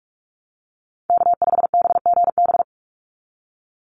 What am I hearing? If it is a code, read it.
Morse code, heard as a keyed tone, about 44 wpm, 700 Hz. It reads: K5BGB
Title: Call Signs Repeated 3x - 45wpm